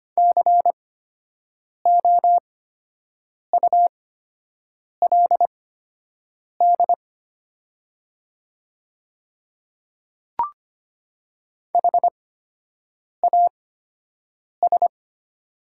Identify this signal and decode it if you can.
Morse code, heard as a keyed tone, about 25 wpm, spaced as for 7 wpm, 700 Hz. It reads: COULD HAS
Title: 8x Character Spacing - Top 100 Words - 25wpm